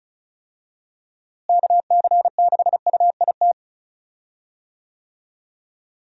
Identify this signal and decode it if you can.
Morse code, heard as a keyed tone, about 34 wpm, 700 Hz. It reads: KC6UIT
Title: USA Call Signs - 6 Characters - Repeated 2x - 35wpm